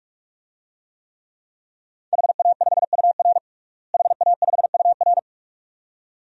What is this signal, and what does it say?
Morse code, heard as a keyed tone, about 46 wpm, 700 Hz. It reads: HA5UR HA5UR
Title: DXCC-100 Call Signs - 5 Characters - Repeated 2x - 45wpm